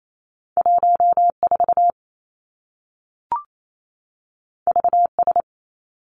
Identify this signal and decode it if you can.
Morse code, heard as a keyed tone, about 28 wpm, 700 Hz. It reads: 1 4 V S
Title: Two Challenging Letter-Number Combinations 28wpm